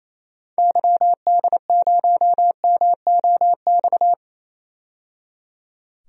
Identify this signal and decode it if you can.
Morse code, heard as a keyed tone, about 28 wpm, 700 Hz. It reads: YD0MOX
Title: DXCC-100 Call Signs - 6 Characters - 28wpm